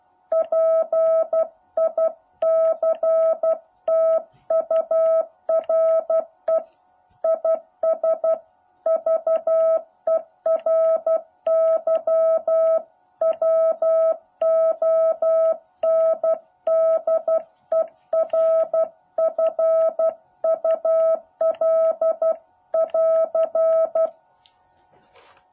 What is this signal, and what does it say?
Morse code, heard as a keyed tone, about 12 wpm, 650 Hz. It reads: PICTURE IS VERY WONDERFUL<AR>
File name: jh8cbh / 2021cw